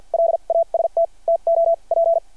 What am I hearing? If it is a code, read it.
Morse code, heard as a keyed tone, about 52 wpm, spaced as for 41 wpm, 660 Hz. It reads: LAST TO P
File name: archived / CRSnet